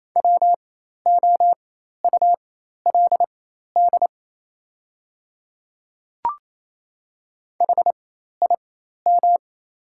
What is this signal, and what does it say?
Morse code, heard as a keyed tone, about 28 wpm, spaced as for 13 wpm, 690 Hz. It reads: WOULD HIM
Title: 4x Character Spacing - Top 200 Words - 28wpm